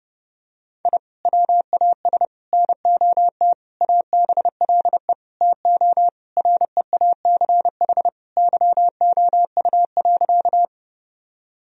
Morse code, heard as a keyed tone, about 32 wpm, 700 Hz. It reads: I WAS NOT ABLE TO REACH YOU.
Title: Sentences from Top 1000 Words - Review Part 2 - 30wpm